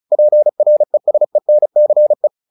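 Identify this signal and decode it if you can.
Morse code, heard as a keyed tone, about 34 wpm, 600 Hz. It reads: PRESENCE
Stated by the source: Morse practice text "Tower Trouble" sent at 35 WPM, from the SKCC - Straight Key Century Club learning center audio files.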